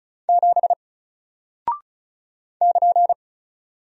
Morse code, heard as a keyed tone, about 36 wpm, 700 Hz. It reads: 7 <KN>
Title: Single Character (Letter:Number:Punctuation:Prosign) Alt 35wpm